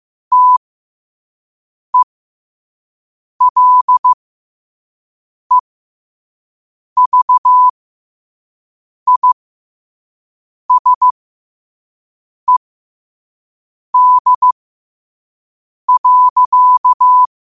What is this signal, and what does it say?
Morse code, heard as a keyed tone, about 15 wpm, spaced as for 5 wpm, 1000 Hz. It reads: TELEVISED.
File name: qotdfiles / 5wpm